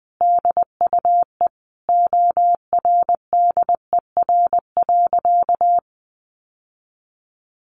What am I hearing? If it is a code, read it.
Morse code, heard as a keyed tone, about 20 wpm, 700 Hz. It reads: DUE ORDER.
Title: Sentences from Top 900 Words 20wpm